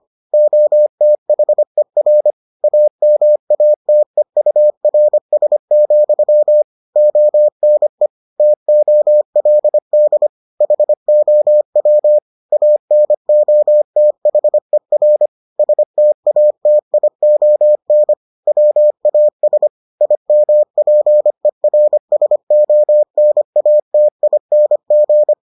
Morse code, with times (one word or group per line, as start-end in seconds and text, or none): 0.3-2.3 OTHER
2.6-6.6 AMATEURS,
7.0-8.1 ONE
8.4-10.3 TOLD
10.6-12.2 HOW
12.5-15.3 ANOTHER
15.6-18.1 STATION
18.5-19.7 WAS
20.0-25.3 IMPERSONATING